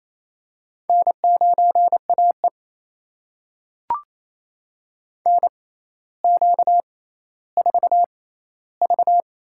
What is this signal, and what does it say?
Morse code, heard as a keyed tone, about 28 wpm, spaced as for 14 wpm, 700 Hz. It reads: N9AE NQ4V
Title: USA Call Signs - 4 Characters - 6x Character Spacing - 28wpm